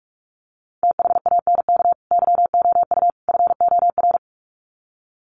Morse code, heard as a keyed tone, about 44 wpm, 700 Hz. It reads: THANK YOU FOR
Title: Sets of 3 Words 45wpm